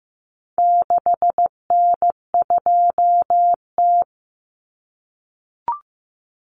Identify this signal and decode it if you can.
Morse code, heard as a keyed tone, about 15 wpm, 700 Hz. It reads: 6N2T
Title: Common Cut Numbers - 4-Digit 15wpm